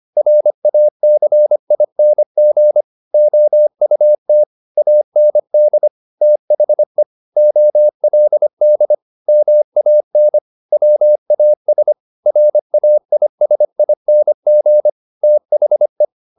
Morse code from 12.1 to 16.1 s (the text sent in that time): RAISING THE